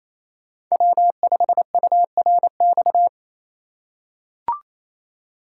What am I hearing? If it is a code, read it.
Morse code, heard as a keyed tone, about 28 wpm, 700 Hz. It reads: W5URX
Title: USA Call Signs - 5 Characters - 2x Character Spacing - 28wpm